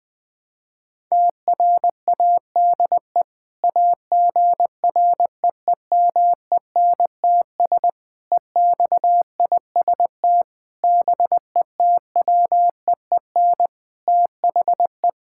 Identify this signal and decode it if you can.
Morse code, heard as a keyed tone, about 21 wpm, 700 Hz. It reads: TRADE AGREEMENTS EXIST BETWEEN THE
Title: Sentences with 5-Word Phrases - 20wpm